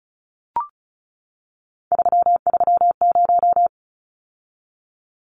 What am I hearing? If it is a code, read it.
Morse code, heard as a keyed tone, about 34 wpm, 700 Hz. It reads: 330
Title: Numbers - 3-Digit 35wpm